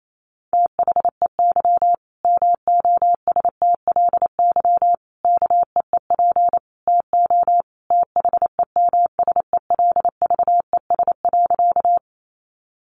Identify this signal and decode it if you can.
Morse code, heard as a keyed tone, about 28 wpm, 700 Hz. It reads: THEY MOSTLY KEEP TO THEMSELVES.
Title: Sentences from Top 1300 Words 28wpm